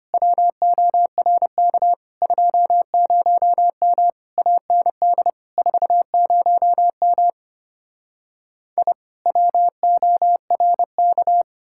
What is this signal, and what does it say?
Morse code, heard as a keyed tone, about 33 wpm, 700 Hz. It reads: WORK 20M AND 40M I WORK